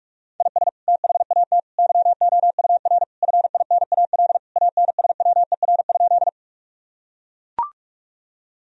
Morse code, heard as a keyed tone, about 41 wpm, 700 Hz. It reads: IS THAT YOUR FINAL ANSWER?
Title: Sentences from Top 1000 Words - Review Part 2 - 45wpm